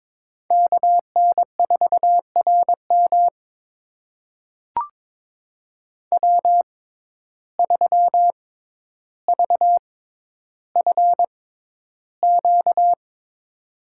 Morse code, heard as a keyed tone, about 22 wpm, spaced as for 11 wpm, 700 Hz. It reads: KN4RM W3VFQ
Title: USA Call Signs - 5 Characters - 6x Character Spacing - 22wpm